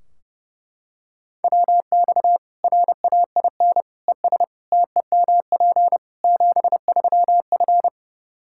Morse code, heard as a keyed tone, about 30 wpm, 700 Hz. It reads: WX RAIN ES TEMP 73F
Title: Practice QSO - Exchange 2 - Info - 30wpm